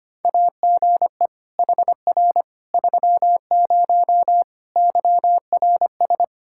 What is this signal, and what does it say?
Morse code, heard as a keyed tone, about 25 wpm, 700 Hz. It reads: AGE HR 30 YRS
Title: Practice QSO - Exchange 3 - More Info - 25wpm